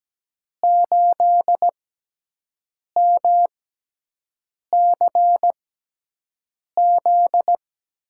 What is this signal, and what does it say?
Morse code, heard as a keyed tone, about 17 wpm, spaced as for 6 wpm, 700 Hz. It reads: 8MCZ
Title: USA Call Signs - 6 Characters - 6x Character Spacing - 17wpm